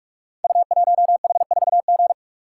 Morse code, heard as a keyed tone, about 44 wpm, 690 Hz. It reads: U1H4G
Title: Code Groups - 5 Letter:Numbers - 45wpm